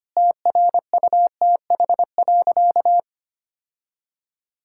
Morse code, heard as a keyed tone, about 25 wpm, 700 Hz. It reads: TRUTH.
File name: practice / 25wpm / Sentences with 9-Word Phrases - 25wpm